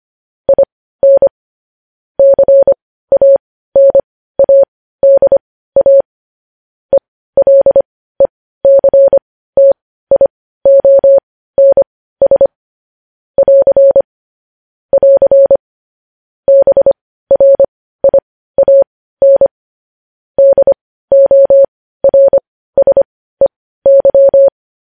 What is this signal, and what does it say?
Morse code, heard as a keyed tone, about 25 wpm, spaced as for 15 wpm, 550 Hz. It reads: IN CANADA ELECTIONS <AR> <AR> BRIAN DORSEY